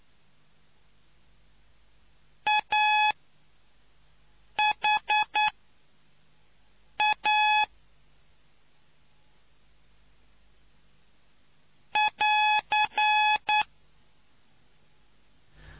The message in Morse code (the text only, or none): AHA <AR>